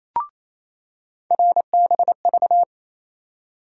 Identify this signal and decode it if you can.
Morse code, heard as a keyed tone, about 28 wpm, 700 Hz. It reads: RBV
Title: USA Call Sign Postfixes - 3 Letters 28wpm